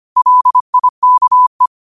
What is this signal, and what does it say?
Morse code, heard as a keyed tone, about 25 wpm, 1000 Hz. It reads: LIKE
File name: qotdfiles / 25wpm